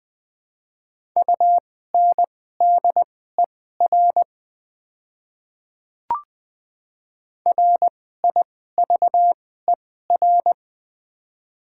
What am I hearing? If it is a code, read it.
Morse code, heard as a keyed tone, about 20 wpm, spaced as for 15 wpm, 700 Hz. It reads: UNDER RIVER
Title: 2x Character Spacing - Top 500 Words - 20wpm